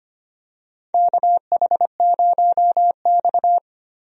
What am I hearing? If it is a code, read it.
Morse code, heard as a keyed tone, about 25 wpm, 700 Hz. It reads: KH0X
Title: USA Call Signs - 4 Characters - Repeated 2x - 25wpm